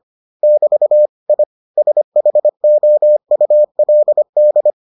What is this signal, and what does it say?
Morse code, heard as a keyed tone, about 25 wpm, 600 Hz. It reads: <BT> I SHOULD